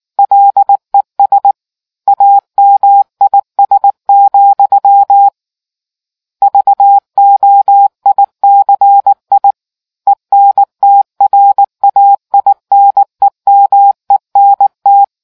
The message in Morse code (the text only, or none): LES AMIS, VOICI ENTRAINEMENT